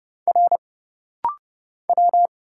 Morse code, heard as a keyed tone, about 31 wpm, 700 Hz. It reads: R W
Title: Single Letters - Rapid-Fire 30wpm